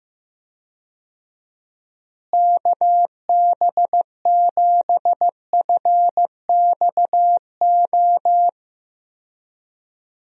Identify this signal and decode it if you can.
Morse code, heard as a keyed tone, about 15 wpm, 700 Hz. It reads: KB7FXO
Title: USA Call Signs - 6 Characters - Repeated 2x - 15wpm